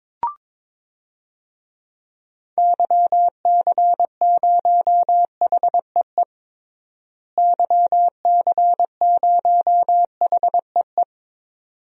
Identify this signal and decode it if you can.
Morse code, heard as a keyed tone, about 22 wpm, 700 Hz. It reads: YC0HEE YC0HEE
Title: DXCC-100 Call Signs - 6 Characters - Repeated 2x - 22wpm